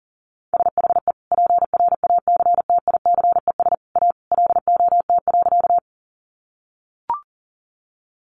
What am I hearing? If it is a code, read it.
Morse code, heard as a keyed tone, about 37 wpm, 700 Hz. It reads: SHE PRACTICES A LOT.
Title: Additional Easy Phrases 40wpm